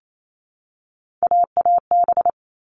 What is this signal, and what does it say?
Morse code, heard as a keyed tone, about 28 wpm, 700 Hz. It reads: AAB
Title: All Cut Numbers - 3-Digit 28wpm